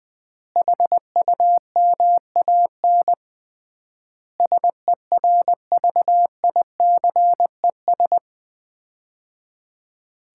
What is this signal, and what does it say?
Morse code, heard as a keyed tone, about 20 wpm, 700 Hz. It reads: HUMAN SERVICES
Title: 3x Word-Spacing - Sets of 5 Words - 20wpm